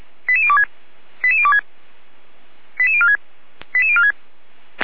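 Five-tone selective call (ZVEI), heard as quick runs of five tones, tones of about 80 ms. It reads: 88026 88026 88046 88046
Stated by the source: ZVEI five-tone selective call received by SDR, archived by Feuerwehr Orlamünde fire brigade.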